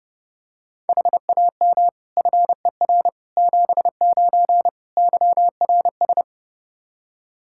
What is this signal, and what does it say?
Morse code, heard as a keyed tone, about 30 wpm, 700 Hz. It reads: HAM FER 79 YRS
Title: Practice QSO - Exchange 3 - More Info - 30wpm